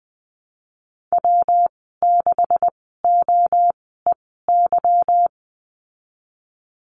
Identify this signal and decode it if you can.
Morse code, heard as a keyed tone, about 19 wpm, spaced as for 14 wpm, 700 Hz. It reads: W6OEY
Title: 167 Lesson 27g - Number 7 - Call Signs 20wpm